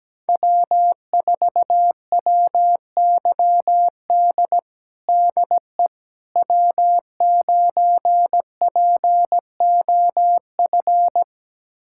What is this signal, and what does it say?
Morse code, heard as a keyed tone, about 17 wpm, 700 Hz. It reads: W4WYD DE W9POF
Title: USA Callsign DE Callsign - 5 Characters - 17wpm